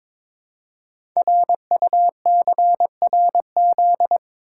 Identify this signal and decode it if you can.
Morse code, heard as a keyed tone, about 22 wpm, 700 Hz. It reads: RUCRZ
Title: Code Groups - 5 Letters - 22wpm